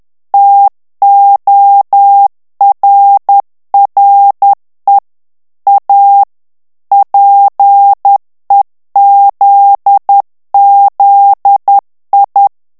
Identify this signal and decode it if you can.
Morse code, heard as a keyed tone, about 10 wpm, 800 Hz. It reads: TORRE A PEZZI
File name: morse mp3 / ITALIEN texte clair 1 vitesse 10 mots minute